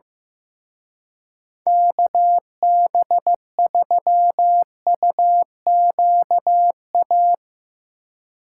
Morse code, heard as a keyed tone, about 15 wpm, 700 Hz. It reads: KB3UQA
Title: Speed Racing - USA 2x3 Call Signs 15wpm